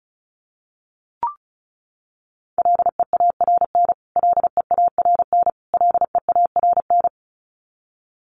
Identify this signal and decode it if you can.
Morse code, heard as a keyed tone, about 34 wpm, 700 Hz. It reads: LEARN LEARN LEARN
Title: Top 300 Words Repeated 35wpm